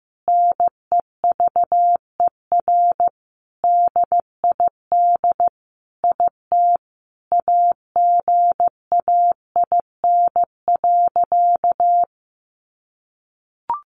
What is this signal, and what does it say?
Morse code, heard as a keyed tone, about 15 wpm, 700 Hz. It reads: NEVER DID IT AGAIN.
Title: Sentences from Top 300 Words 15wpm